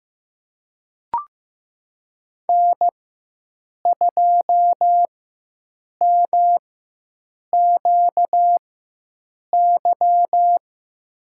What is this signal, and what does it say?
Morse code, heard as a keyed tone, about 15 wpm, spaced as for 7 wpm, 700 Hz. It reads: N2MQY
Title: USA Call Signs - 5 Characters - 4x Character Spacing - 15wpm